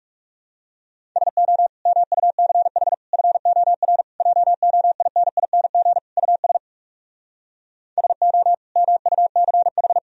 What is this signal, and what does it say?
Morse code, heard as a keyed tone, about 40 wpm, 700 Hz. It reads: SO MUCH FOR JOINING US SO MUCH